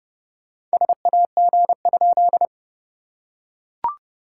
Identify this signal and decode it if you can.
Morse code, heard as a keyed tone, about 29 wpm, 690 Hz. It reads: SAG?
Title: SOTA QSO Practice 30wpm